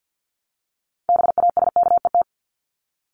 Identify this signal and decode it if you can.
Morse code, heard as a keyed tone, about 50 wpm, 700 Hz. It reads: BASKET
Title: Top 1751-2000 Words 50wpm